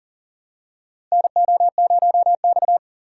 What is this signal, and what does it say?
Morse code, heard as a keyed tone, about 40 wpm, 700 Hz. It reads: NO0X
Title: USA Call Signs - 4 Characters - 2x Character Spacing - 40wpm